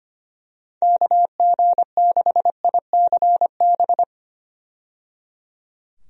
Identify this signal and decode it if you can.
Morse code, heard as a keyed tone, about 25 wpm, 700 Hz. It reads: KG6ICB
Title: USA Call Signs - 6 Characters - 25wpm